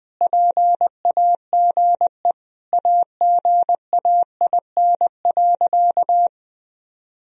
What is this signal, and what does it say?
Morse code, heard as a keyed tone, about 20 wpm, 700 Hz. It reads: PAGE AGAIN.
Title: Sentences from Top 1000 Words - Review Part 1 - 20wpm